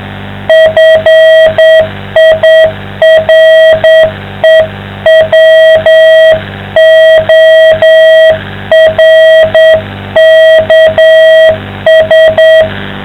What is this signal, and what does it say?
Morse code, heard as a keyed tone, about 9 wpm, 660 Hz. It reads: FIREWORKS